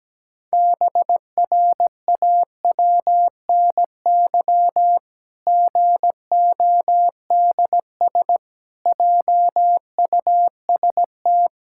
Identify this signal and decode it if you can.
Morse code, heard as a keyed tone, about 17 wpm, 700 Hz. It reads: BRAWNY GODS JUST
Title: Pangram Challenge v2 - 17wpm